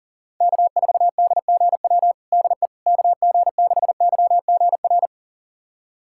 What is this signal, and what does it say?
Morse code, heard as a keyed tone, about 40 wpm, 700 Hz. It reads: K4DGW DE KG6YGR
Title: USA Callsign DE Callsign 40wpm